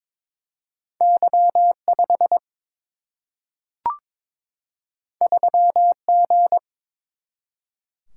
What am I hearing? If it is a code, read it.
Morse code, heard as a keyed tone, about 22 wpm, 700 Hz. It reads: Y 5 3 G
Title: Single-Letter Single-Number 22wpm